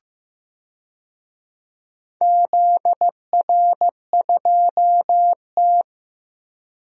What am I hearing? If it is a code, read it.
Morse code, heard as a keyed tone, about 15 wpm, 700 Hz. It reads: ZR2T
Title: DXCC-100 Call Signs - 4 Characters - Repeated 3x - 15wpm